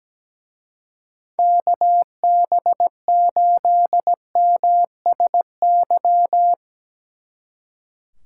Morse code, heard as a keyed tone, about 19 wpm, 700 Hz. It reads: KB8MSY
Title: Speed Racing - USA Call Signs 17wpm